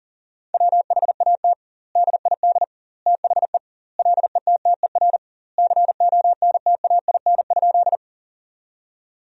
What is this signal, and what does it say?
Morse code, heard as a keyed tone, about 40 wpm, 700 Hz. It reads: WHAT DID THE LETTER CONTAIN?
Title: 2x Word-Spacing - Sentences from Top 500 Words - 40wpm